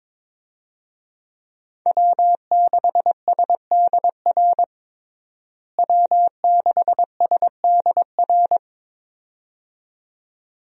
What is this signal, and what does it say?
Morse code, heard as a keyed tone, about 22 wpm, 700 Hz. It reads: W6SDR W6SDR
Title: USA Call Signs - 5 Characters - Repeated 2x - 22wpm